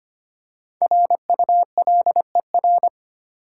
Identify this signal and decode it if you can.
Morse code, heard as a keyed tone, about 25 wpm, 700 Hz. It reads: RULER
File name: practice / 25wpm / Easy Words with Hints - 25wpm